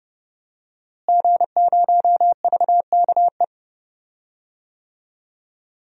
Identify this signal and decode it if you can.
Morse code, heard as a keyed tone, about 30 wpm, 700 Hz. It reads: G0VKE
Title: DXCC-100 Call Signs - 5 Characters - Repeated 3x - 30wpm